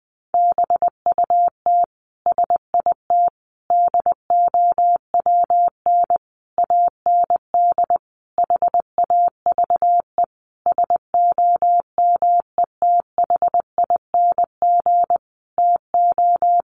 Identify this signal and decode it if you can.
Morse code, heard as a keyed tone, about 20 wpm, 700 Hz. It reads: BUT SIT DOWN AND HAVE SOMETHING TO
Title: Sentences from Top 600 Words 20wpm